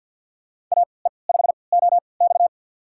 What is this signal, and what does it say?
Morse code, heard as a keyed tone, about 51 wpm, spaced as for 28 wpm, 700 Hz. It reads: AE5OX
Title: USA Call Signs - 5 Characters - 3x Character Spacing - 50wpm